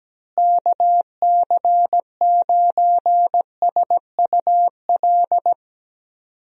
Morse code, heard as a keyed tone, about 17 wpm, 700 Hz. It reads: KC9SUL
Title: USA Call Signs - 6 Characters - Repeated 2x - 17wpm